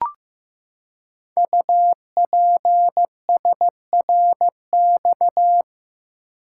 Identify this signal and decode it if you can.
Morse code, heard as a keyed tone, about 15 wpm, 700 Hz. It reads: UPSRX
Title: Code Groups - 5 Letters - 15wpm